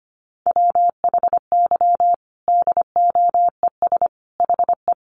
Morse code, heard as a keyed tone, about 25 wpm, 700 Hz. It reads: WHY DOES HE
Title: Sentences from Top 1100 Words 25wpm